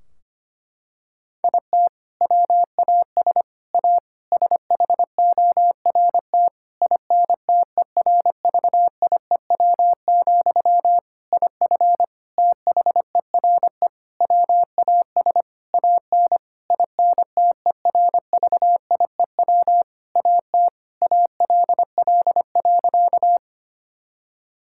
Morse code, heard as a keyed tone, about 25 wpm, 700 Hz. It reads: IT WAS A SHORT INTERVIEW, IF THERE WAS AN INTERVIEW AT ALL.
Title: Sentences from Top 1600 Words 25wpm